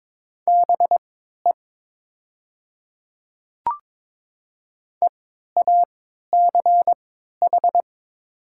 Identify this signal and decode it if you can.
Morse code, heard as a keyed tone, about 22 wpm, spaced as for 12 wpm, 700 Hz. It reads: BE EACH
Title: 3x Character Spacing - Top 100 Words - 22wpm